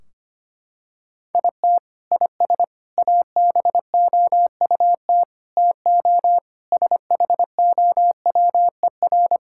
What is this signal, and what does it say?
Morse code, heard as a keyed tone, about 25 wpm, 700 Hz. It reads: IT IS ABOUT TO SHOWER
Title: Sentences from Top 1600 Words 25wpm